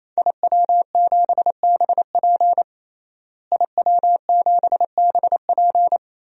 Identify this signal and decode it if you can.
Morse code, heard as a keyed tone, about 28 wpm, 700 Hz. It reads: IW7BP IW7BP
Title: DXCC-100 Call Signs - 5 Characters - Repeated 3x - 28wpm